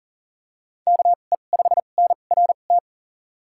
Morse code, heard as a keyed tone, about 40 wpm, spaced as for 29 wpm, 700 Hz. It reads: KE5NRT
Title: USA Call Signs - 2x Character Spacing - 40wpm